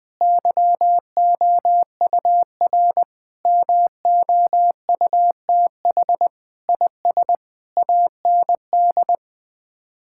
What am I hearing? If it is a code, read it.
Morse code, heard as a keyed tone, about 19 wpm, 700 Hz. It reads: YOUR MOUTH IS AND
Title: Sets of 8 Words - 20wpm